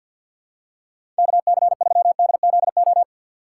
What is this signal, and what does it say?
Morse code, heard as a keyed tone, about 52 wpm, 700 Hz. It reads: KC3DZO
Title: USA Call Signs - 6 Characters - 3x Character Spacing - 50wpm